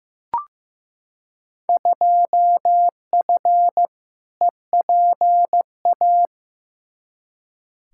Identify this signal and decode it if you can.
Morse code, heard as a keyed tone, about 15 wpm, 700 Hz. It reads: 2F EPA
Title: Field Day Exchanges - 15wpm